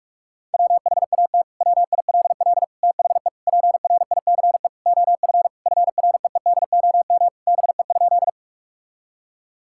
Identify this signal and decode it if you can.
Morse code, heard as a keyed tone, about 41 wpm, 700 Hz. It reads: WHAT WILL THE PRICE OF FREEDOM BE?
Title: Sentences from Top 1000 Words - Review Part 5 - 45wpm